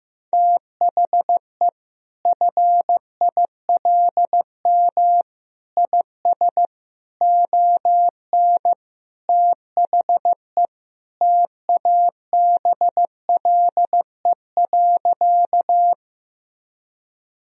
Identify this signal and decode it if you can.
Morse code, heard as a keyed tone, about 15 wpm, 700 Hz. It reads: THE FILM IS ON THE TABLE.
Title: Sentences from Top 1000 Words - Review Part 3 - 15wpm